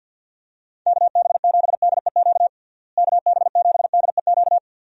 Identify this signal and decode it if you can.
Morse code, heard as a keyed tone, about 51 wpm, 700 Hz. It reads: KB7DEQ KB7DEQ
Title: USA Call Signs - 6 Characters - Repeated 2x - 50wpm